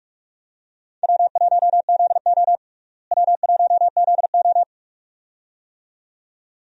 Morse code, heard as a keyed tone, about 43 wpm, 700 Hz. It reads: W1ZO W1ZO
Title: USA Call Signs - 4 Characters - Repeated 2x - 45wpm